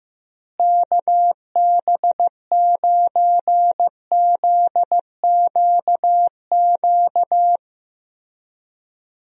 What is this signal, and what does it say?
Morse code, heard as a keyed tone, about 15 wpm, 700 Hz. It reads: KB9ZQQ
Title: USA Call Signs - 6 Characters - Repeated 2x - 15wpm